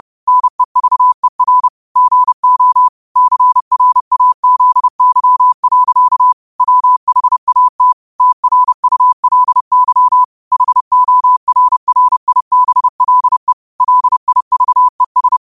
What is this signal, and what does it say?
Morse code, heard as a keyed tone, about 30 wpm, 1000 Hz. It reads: NEVER GO CRAZY. WHAT TRULY HORRIBLE LIVES
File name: qotdfiles / 30wpm